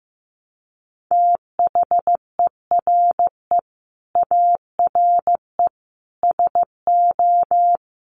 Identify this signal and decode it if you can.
Morse code, heard as a keyed tone, about 15 wpm, 700 Hz. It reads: THERE ARE SO
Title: Sets of 4 Words 15wpm